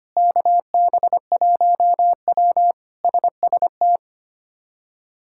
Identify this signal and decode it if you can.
Morse code, heard as a keyed tone, about 25 wpm, 700 Hz. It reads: KB1W SST
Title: K1USN SST QSO Practice - 25wpm